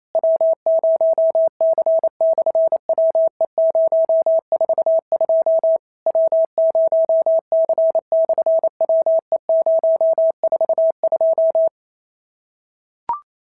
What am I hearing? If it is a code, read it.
Morse code, heard as a keyed tone, about 28 wpm, 650 Hz. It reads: W0C/WE042 W0C/WE042
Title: SOTA QSO Practice 28wpm